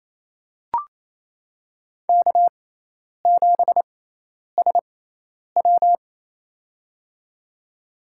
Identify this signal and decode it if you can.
Morse code, heard as a keyed tone, about 28 wpm, spaced as for 10 wpm, 700 Hz. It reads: K7SW
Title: USA Call Signs - 6x Character Spacing - 28wpm